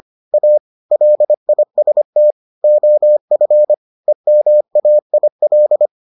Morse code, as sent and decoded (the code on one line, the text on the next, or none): .- / .-.. .. ... - / --- ..-. / . -- .- .. .-..
A LIST OF EMAIL